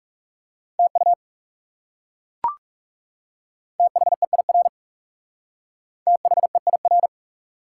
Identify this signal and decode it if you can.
Morse code, heard as a keyed tone, about 37 wpm, 700 Hz. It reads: TU THEIR THEIR
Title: Speed-Racing Top 180 QSO Elements 30wpm